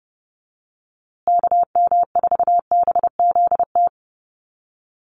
Morse code, heard as a keyed tone, about 30 wpm, 700 Hz. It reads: KM4BZT
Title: USA Call Signs Repeated 2x 30wpm